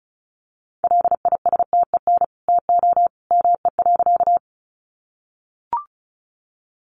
Morse code, heard as a keyed tone, about 34 wpm, 700 Hz. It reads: LISTEN TO ME.
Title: Sentences from Top 400 Words 35wpm